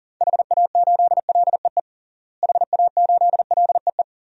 Morse code, heard as a keyed tone, about 40 wpm, 700 Hz. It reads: HA8LEE HA8LEE
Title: DXCC-100 Call Signs - 6 Characters - Repeated 2x - 40wpm